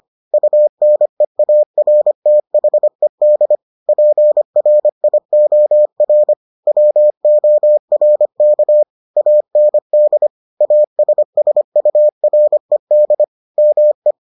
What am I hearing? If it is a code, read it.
Morse code, heard as a keyed tone, about 25 wpm, 600 Hz. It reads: UNEARTHED PRIOR WORK AND ASSURED ME